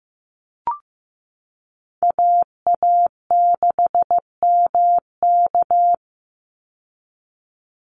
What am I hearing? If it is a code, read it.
Morse code, heard as a keyed tone, about 15 wpm, 700 Hz. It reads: AA6MK
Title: Call Signs Repeated 2x - 15wpm